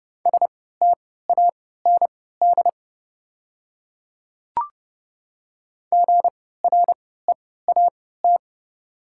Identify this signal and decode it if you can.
Morse code, heard as a keyed tone, about 30 wpm, spaced as for 17 wpm, 700 Hz. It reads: STAND GREAT
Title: 3x Character Spacing - Top 300 Words - 30wpm